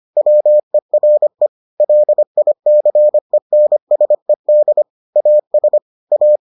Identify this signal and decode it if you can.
Morse code, heard as a keyed tone, about 25 wpm, 600 Hz. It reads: WERE LICENSED AS A